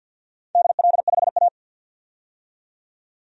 Morse code, heard as a keyed tone, about 53 wpm, 700 Hz. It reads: DL5A
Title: DXCC-100 Call Signs - 4 Characters - 50wpm